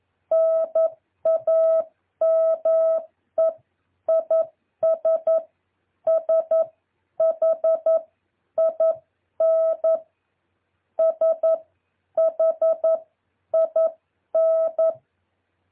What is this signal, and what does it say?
Morse code, heard as a keyed tone, about 11 wpm, spaced as for 9 wpm, 650 Hz. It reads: NAME IS SHIN SHIN
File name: jh8cbh / 2023cw